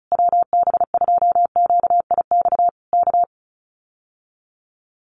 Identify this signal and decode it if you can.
Morse code, heard as a keyed tone, about 34 wpm, 700 Hz. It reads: WB2QIX K
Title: Common QSO Phrases Encore 35wpm